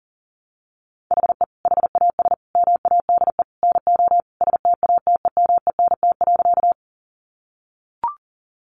Morse code, heard as a keyed tone, about 37 wpm, 700 Hz. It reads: HE HAS MADE NO STATEMENT.
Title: Sentences from Top 1200 Words 40wpm